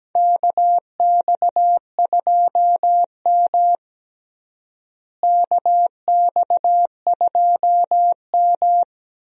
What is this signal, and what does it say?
Morse code, heard as a keyed tone, about 17 wpm, 700 Hz. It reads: KX2M KX2M
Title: USA Call Signs - 4 Characters - Repeated 2x - 17wpm